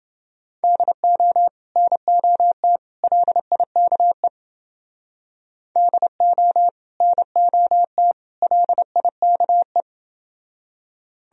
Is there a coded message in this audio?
Yes: Morse code from 0.6 to 9.8 s, heard as a keyed tone, about 28 wpm, 700 Hz. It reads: DO NOT LIKE DO NOT LIKE